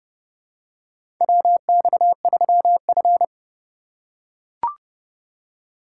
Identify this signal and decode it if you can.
Morse code, heard as a keyed tone, about 32 wpm, 700 Hz. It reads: WX3F
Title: USA Call Signs - 4 Characters - 2x Character Spacing - 30wpm